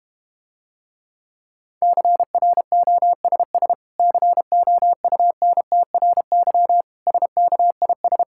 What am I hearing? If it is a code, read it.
Morse code, heard as a keyed tone, about 32 wpm, 700 Hz. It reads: CROSS COUNTRY SKIS